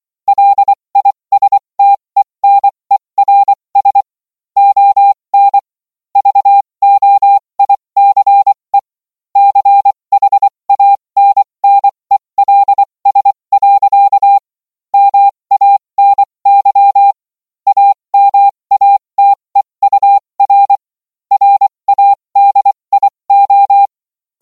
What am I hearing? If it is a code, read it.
Morse code, heard as a keyed tone, about 24 wpm, spaced as for 20 wpm, 800 Hz. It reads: LISTENERS ON VOICE CHANNELS. MANY AMATEUR RADIO